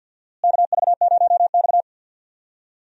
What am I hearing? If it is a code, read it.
Morse code, heard as a keyed tone, about 51 wpm, 700 Hz. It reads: KV0X
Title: USA Call Signs - 4 Characters - Repeated 2x - 50wpm